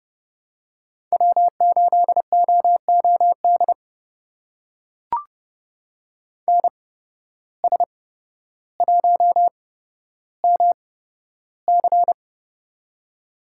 Morse code, heard as a keyed tone, about 30 wpm, spaced as for 13 wpm, 700 Hz. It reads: W8OOD NS1MC
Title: USA Call Signs - 5 Characters - 8x Character Spacing - 30wpm